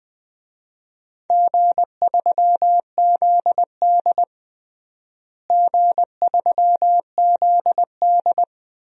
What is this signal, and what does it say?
Morse code, heard as a keyed tone, about 20 wpm, 700 Hz. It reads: G3ZD G3ZD
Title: DXCC-100 Call Signs - 4 Characters - Repeated 3x - 20wpm